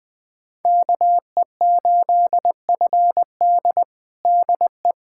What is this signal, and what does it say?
Morse code, heard as a keyed tone, about 20 wpm, 700 Hz. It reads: KE8FD DE
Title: USA Callsign DE Callsign - 5 Characters - 20wpm